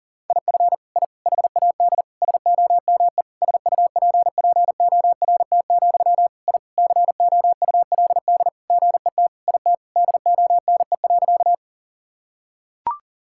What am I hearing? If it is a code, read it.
Morse code, heard as a keyed tone, about 40 wpm, 700 Hz. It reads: IF I HAD SOME SUPPORT, I COULD GET IT DONE.
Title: Sentences from Top 1000 Words - Review Part 5 - 40wpm